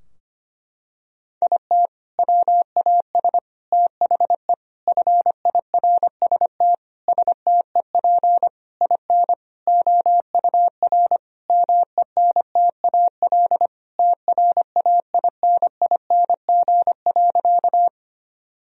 Morse code, heard as a keyed tone, about 25 wpm, 700 Hz. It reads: IT WAS THE FIRST STEP IN OUR MENTAL TRAINING.
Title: Sentences from Top 1000 Words - Review Part 4 - 25wpm